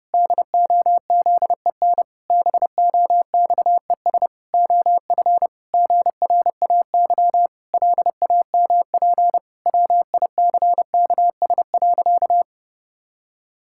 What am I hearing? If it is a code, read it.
Morse code, heard as a keyed tone, about 32 wpm, 700 Hz. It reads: DOZEN BOXES OF GRAY LAMP WICKS.
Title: Pangram Challenge v2 - 30wpm